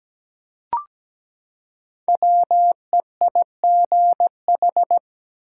Morse code, heard as a keyed tone, about 17 wpm, 700 Hz. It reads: WEIGH
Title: Five Letter Words - 17wpm